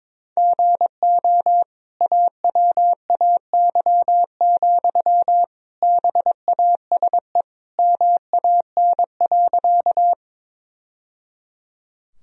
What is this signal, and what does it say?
Morse code, heard as a keyed tone, about 22 wpm, 700 Hz. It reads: GO AWAY, BASE MAN.
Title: Sentences from Top 1000 Words - Review Part 1 - 22wpm